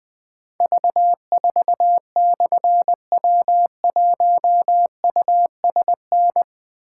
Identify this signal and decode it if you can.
Morse code, heard as a keyed tone, about 20 wpm, 700 Hz. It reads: V4/W1USN
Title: Difficult Call Signs 20wpm